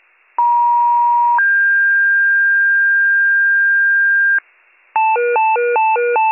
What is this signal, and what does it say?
Two-tone page, heard as a one-second tone, then a three-second tone, about 950 then 1600 Hz, with noise. It is heard from 0.4 to 4.4 s.